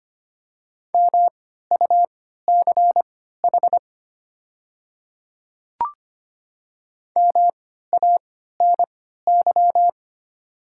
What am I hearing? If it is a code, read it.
Morse code, heard as a keyed tone, about 25 wpm, spaced as for 14 wpm, 700 Hz. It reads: MUCH MANY
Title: 3x Character Spacing - Top 500 Words - 25wpm